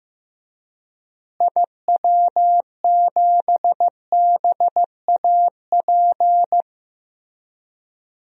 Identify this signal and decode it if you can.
Morse code, heard as a keyed tone, about 15 wpm, 700 Hz. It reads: IW7BAP
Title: DXCC-100 Call Signs - 6 Characters - Repeated 3x - 15wpm